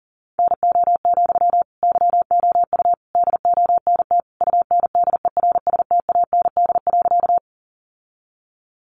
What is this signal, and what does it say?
Morse code, heard as a keyed tone, about 40 wpm, 700 Hz. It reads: NO, YOU DONT UNDERSTAND.
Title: Sentences from Top 600 Words 40wpm